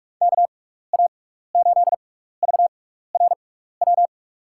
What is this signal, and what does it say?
Morse code, heard as a keyed tone, about 46 wpm, spaced as for 15 wpm, 700 Hz. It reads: KA8VRW
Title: USA Call Signs - 6 Characters - 6x Character Spacing - 45wpm